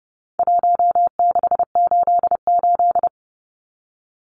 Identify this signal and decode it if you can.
Morse code, heard as a keyed tone, about 30 wpm, 700 Hz. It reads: 1688
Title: Numbers - Contest Style - 4-Digit 30wpm